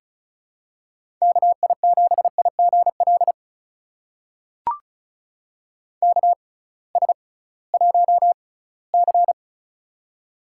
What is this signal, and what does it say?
Morse code, heard as a keyed tone, about 36 wpm, spaced as for 20 wpm, 700 Hz. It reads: KI7IGL KS1C
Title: USA Call Signs - 6x Character Spacing - 35wpm